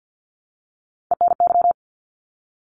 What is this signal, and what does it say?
Morse code, heard as a keyed tone, about 50 wpm, 700 Hz. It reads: ENY
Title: ARRL Sections 50wpm